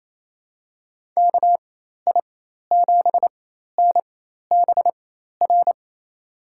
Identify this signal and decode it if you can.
Morse code, heard as a keyed tone, about 28 wpm, spaced as for 13 wpm, 700 Hz. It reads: KI7NBR
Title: USA Call Signs - 4x Character Spacing - 28wpm